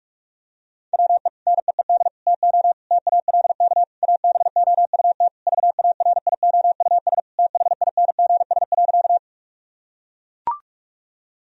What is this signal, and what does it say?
Morse code, heard as a keyed tone, about 42 wpm, 700 Hz. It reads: WE NEED TO TALK ABOUT VARIOUS THINGS.
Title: Sentences from Top 1000 Words - Review Part 4 - 45wpm